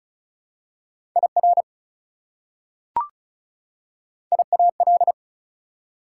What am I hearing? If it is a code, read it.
Morse code, heard as a keyed tone, about 36 wpm, 700 Hz. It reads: IR IAL
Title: Prefixes and Suffixes 35wpm